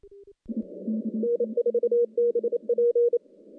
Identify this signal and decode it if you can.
Morse code, heard as a keyed tone, about 28 wpm, 480 Hz. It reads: N4BP
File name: contests / CW-SS-k5tr- -1200